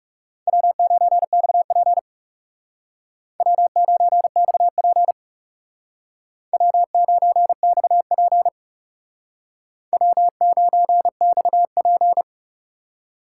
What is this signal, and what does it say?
Morse code, heard as a keyed tone, about 38 wpm, 700 Hz. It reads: W9XP W9XP W9XP W9XP